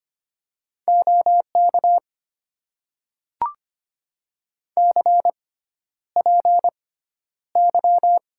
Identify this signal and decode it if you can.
Morse code, heard as a keyed tone, about 25 wpm, spaced as for 11 wpm, 700 Hz. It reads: OK CPY
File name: Top 90 QSO Elements - 6x Character Spacing - 25wpm